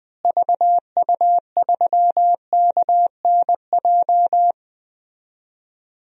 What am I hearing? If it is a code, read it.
Morse code, heard as a keyed tone, about 20 wpm, 700 Hz. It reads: VU3KNJ
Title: DXCC-100 Call Signs - 6 Characters - Repeated 2x - 20wpm